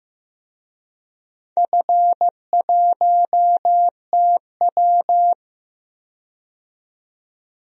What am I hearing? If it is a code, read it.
Morse code, heard as a keyed tone, about 16 wpm, 700 Hz. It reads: F1TW
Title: DXCC-100 Call Signs - 4 Characters - 15wpm